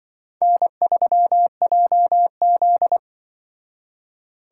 Morse code, heard as a keyed tone, about 24 wpm, 700 Hz. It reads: N3JZ
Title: Speed Racing - USA Call Signs 20wpm